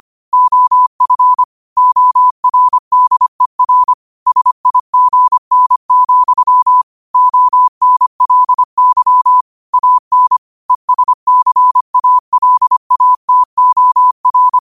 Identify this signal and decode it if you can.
Morse code, heard as a keyed tone, about 25 wpm, 1000 Hz. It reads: OF ORDER SIGN, ONLY AN ESCALATOR
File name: qotdfiles / 25wpm